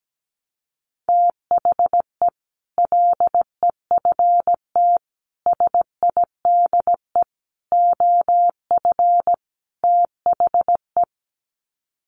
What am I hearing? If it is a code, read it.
Morse code, heard as a keyed tone, about 17 wpm, 700 Hz. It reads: THE LEFT SIDE OF THE
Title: Sets of 5 Words Encore 17wpm